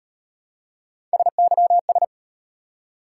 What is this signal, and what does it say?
Morse code, heard as a keyed tone, about 35 wpm, 700 Hz. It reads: SYS